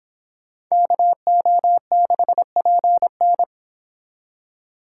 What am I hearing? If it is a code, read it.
Morse code, heard as a keyed tone, about 26 wpm, 700 Hz. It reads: KO6PN